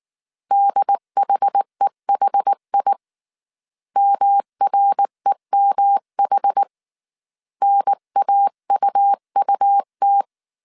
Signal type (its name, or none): Morse code